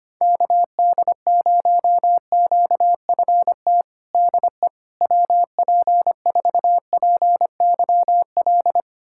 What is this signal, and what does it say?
Morse code, heard as a keyed tone, about 25 wpm, 700 Hz. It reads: KD0QFT DE WP4PYL
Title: USA Callsign DE Callsign - 6 Characters - 25wpm